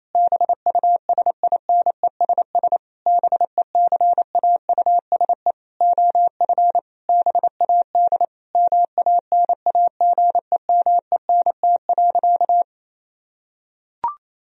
Morse code, heard as a keyed tone, about 28 wpm, 700 Hz. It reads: BUSINESS BECAUSE OF BAD MANAGEMENT.
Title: Sentences from Top 1000 Words - Review Part 4 - 28wpm